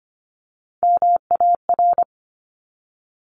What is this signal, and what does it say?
Morse code, heard as a keyed tone, about 25 wpm, 700 Hz. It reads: MAR
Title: ARRL Sections 25wpm